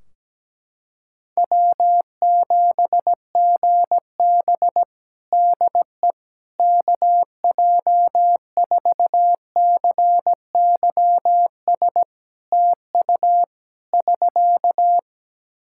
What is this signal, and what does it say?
Morse code, heard as a keyed tone, about 17 wpm, 700 Hz. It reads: W7GB DE KJ4CYS TU <SK>
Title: Practice QSO - Exchange 4 - Ending - 17wpm